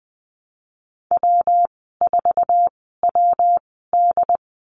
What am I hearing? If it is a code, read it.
Morse code, heard as a keyed tone, about 20 wpm, spaced as for 15 wpm, 700 Hz. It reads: W4WD
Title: 200 Lesson 32g - Letter k - Call Signs 20wpm